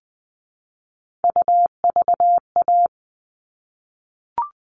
Morse code, heard as a keyed tone, about 20 wpm, 700 Hz. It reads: UVA
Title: All Cut Numbers - 3-Digit 20wpm